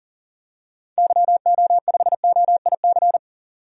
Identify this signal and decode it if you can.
Morse code, heard as a keyed tone, about 40 wpm, 700 Hz. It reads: YO5OIC
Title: DXCC-100 Call Signs - 6 Characters - Repeated 2x - 40wpm